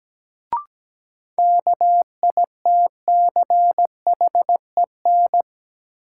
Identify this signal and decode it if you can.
Morse code, heard as a keyed tone, about 17 wpm, 700 Hz. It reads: KITCHEN
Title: Sentences from Top 1400 Words - New Word Drill - 17wpm